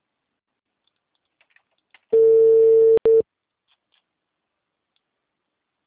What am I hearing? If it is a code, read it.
Ringback tone (China, 450 Hz).